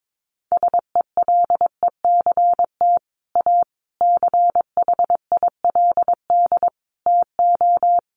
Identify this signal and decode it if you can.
Morse code, heard as a keyed tone, about 22 wpm, 700 Hz. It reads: SELECT A CHILD TO
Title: Sentences from Top 1300 Words 22wpm